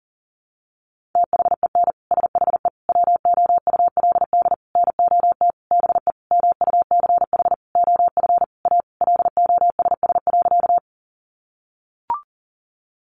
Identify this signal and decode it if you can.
Morse code, heard as a keyed tone, about 40 wpm, 700 Hz. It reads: THEN SHE WOULD NOT BE MUCH OF A LOSS.
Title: Sentences from Top 1400 Words 40wpm